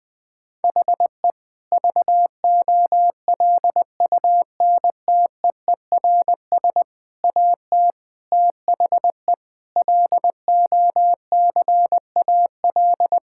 Morse code, heard as a keyed tone, about 21 wpm, 700 Hz. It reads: HE VOLUNTEERS AT THE LOCAL
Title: Difficult Sentences with Hints - 20wpm